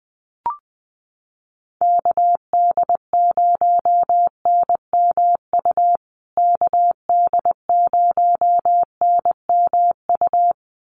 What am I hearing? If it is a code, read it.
Morse code, heard as a keyed tone, about 20 wpm, 700 Hz. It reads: KD0NMU KD0NMU
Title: USA Call Signs Repeated 2x 20wpm